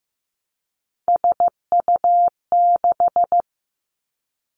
Morse code, heard as a keyed tone, about 15 wpm, 700 Hz. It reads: SU6
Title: Three Challenging Letter-Number Combinations 15wpm